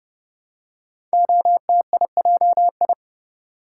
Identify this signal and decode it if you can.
Morse code, heard as a keyed tone, about 30 wpm, 700 Hz. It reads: OTIJI
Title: Code Groups - 5 Letters - 30wpm